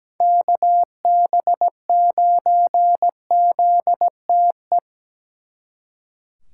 Morse code, heard as a keyed tone, about 17 wpm, 700 Hz. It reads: KB9ZTE
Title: Speed Racing - USA Call Signs 17wpm